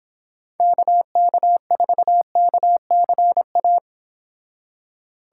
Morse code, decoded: KK4KCA

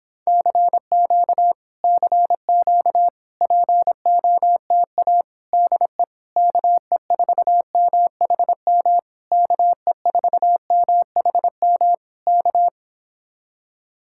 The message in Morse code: CQ CQ POTA DE KE4MHM KE4MHM K

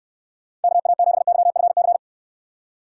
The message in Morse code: KA7YFC

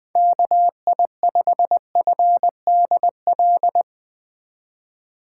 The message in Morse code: KI5FDL

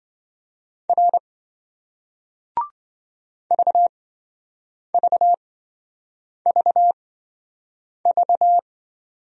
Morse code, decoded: R VVVV

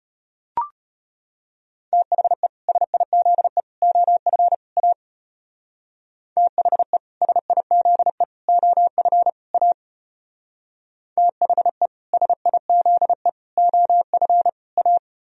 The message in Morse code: THE SIZE OF A THE SIZE OF A THE SIZE OF A

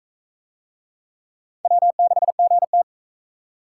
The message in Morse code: W6GT

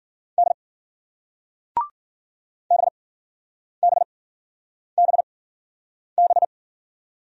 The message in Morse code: D BBBB

